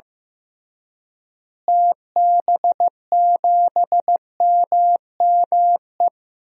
Morse code, heard as a keyed tone, about 16 wpm, 700 Hz. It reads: TB7MME
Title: DXCC-100 Call Signs - 6 Characters - Repeated 3x - 15wpm